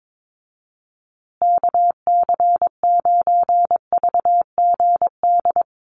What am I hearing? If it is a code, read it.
Morse code, heard as a keyed tone, about 22 wpm, 700 Hz. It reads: KC9VGD
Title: USA Call Signs Repeated 3x 22wpm